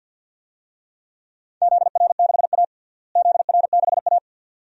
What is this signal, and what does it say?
Morse code, heard as a keyed tone, about 48 wpm, 700 Hz. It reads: ZR6A ZR6A
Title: DXCC-100 Call Signs - 4 Characters - Repeated 2x - 50wpm